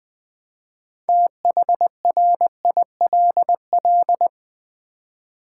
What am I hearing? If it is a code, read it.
Morse code, heard as a keyed tone, about 20 wpm, 700 Hz. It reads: THRILL
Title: Build A Word - 3 Letters At A Time - 8-Letter Words - 20wpm